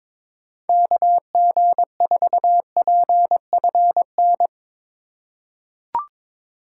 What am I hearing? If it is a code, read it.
Morse code, heard as a keyed tone, about 22 wpm, 700 Hz. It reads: KG4PFN
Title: USA Call Signs - 2x Character Spacing - 22wpm